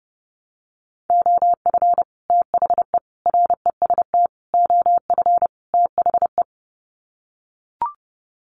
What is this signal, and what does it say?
Morse code, heard as a keyed tone, about 30 wpm, 700 Hz. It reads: OF THE REST OF THE
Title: Sets of 5 Words 30wpm